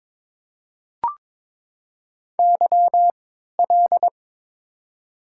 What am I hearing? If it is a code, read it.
Morse code, heard as a keyed tone, about 22 wpm, spaced as for 13 wpm, 700 Hz. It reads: YL
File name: Top 90 QSO Elements - 3x Character Spacing - 22wpm